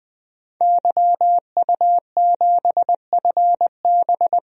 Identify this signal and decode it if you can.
Morse code, heard as a keyed tone, about 20 wpm, 700 Hz. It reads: YU7FB
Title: DXCC-100 Call Signs - 5 Characters - 20wpm